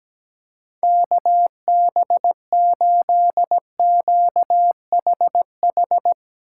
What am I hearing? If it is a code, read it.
Morse code, heard as a keyed tone, about 17 wpm, 700 Hz. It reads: KB8QHH
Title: Speed Racing - USA 2x3 Call Signs 17wpm